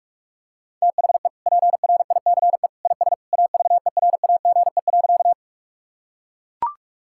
Morse code, heard as a keyed tone, about 41 wpm, 700 Hz. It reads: THE PRICE IS AVERAGE.
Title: Sentences from Top 1000 Words - Review Part 6 - 45wpm